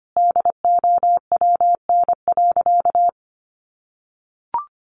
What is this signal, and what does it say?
Morse code, heard as a keyed tone, about 25 wpm, 700 Hz. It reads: DOWN.
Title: Additional Easy Phrases 25wpm